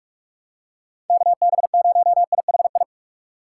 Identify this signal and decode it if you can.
Morse code, heard as a keyed tone, about 44 wpm, 700 Hz. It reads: KB0IHI
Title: USA Call Signs - 6 Characters - 2x Character Spacing - 45wpm